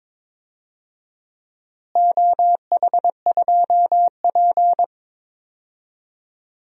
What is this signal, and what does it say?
Morse code, heard as a keyed tone, about 22 wpm, 700 Hz. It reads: OH2P
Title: DXCC-100 Call Signs - 4 Characters - Repeated 3x - 22wpm